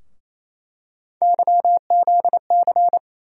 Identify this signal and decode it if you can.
Morse code, heard as a keyed tone, about 27 wpm, 700 Hz. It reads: YZC
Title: USA Call Sign Postfixes - 3 Letters 28wpm